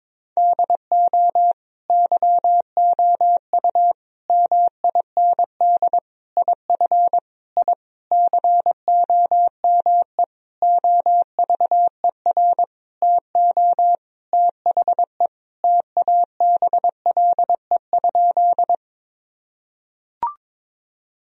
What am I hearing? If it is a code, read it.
Morse code, heard as a keyed tone, about 22 wpm, 700 Hz. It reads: DO YOU MIND IF I COME OVER TO THE TABLE?
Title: Sentences from Top 1000 Words - Review Part 2 - 22wpm